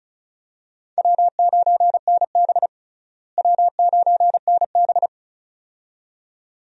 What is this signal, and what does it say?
Morse code, heard as a keyed tone, about 35 wpm, 700 Hz. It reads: W9NB W9NB
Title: USA Call Signs - 4 Characters - Repeated 2x - 35wpm